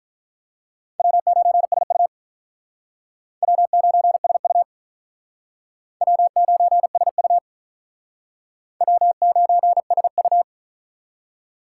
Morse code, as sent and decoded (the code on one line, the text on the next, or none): .-- ----. ... ..- / .-- ----. ... ..- / .-- ----. ... ..- / .-- ----. ... ..-
W9SU W9SU W9SU W9SU